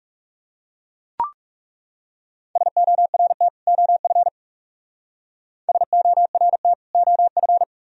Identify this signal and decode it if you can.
Morse code, heard as a keyed tone, about 37 wpm, 700 Hz. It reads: SORT OF SORT OF
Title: Speed-Racing - Sets of 2 Words 30wpm